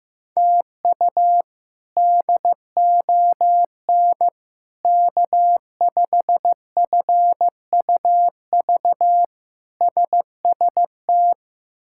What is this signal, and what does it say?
Morse code, heard as a keyed tone, about 15 wpm, 700 Hz. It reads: TU DON K5FUV SST
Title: K1USN SST QSO Practice - 15wpm